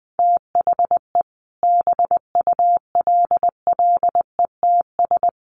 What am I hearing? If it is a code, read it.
Morse code, heard as a keyed tone, about 20 wpm, 700 Hz. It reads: THE BULLETS
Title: A Sound Of Thunder 20wpm